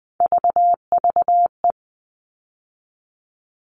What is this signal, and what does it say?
Morse code, heard as a keyed tone, about 20 wpm, 700 Hz. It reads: VVE
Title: All Cut Numbers - 3-Digit 20wpm